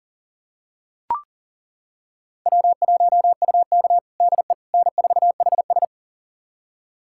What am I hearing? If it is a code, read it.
Morse code, heard as a keyed tone, about 40 wpm, 700 Hz. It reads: W1UK DE N4HS
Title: USA Callsign DE Callsign - 4 Characters - 40wpm